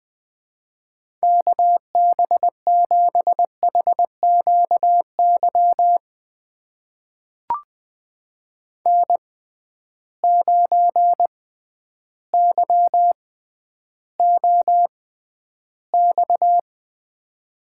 Morse code, heard as a keyed tone, about 20 wpm, spaced as for 11 wpm, 700 Hz. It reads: KB7HQY N9YOX
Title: USA Call Signs - 6x Character Spacing - 20wpm